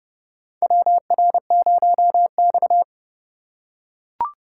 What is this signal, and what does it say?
Morse code, heard as a keyed tone, about 30 wpm, 700 Hz. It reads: WR0X
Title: USA Call Signs - 4x Character Spacing - 30wpm